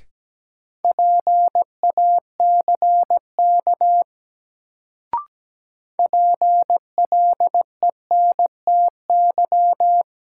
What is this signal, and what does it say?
Morse code, heard as a keyed tone, about 17 wpm, 700 Hz. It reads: PACK PLENTY
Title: Sentences from Top 1300 Words - New Word Drill Alt - 17wpm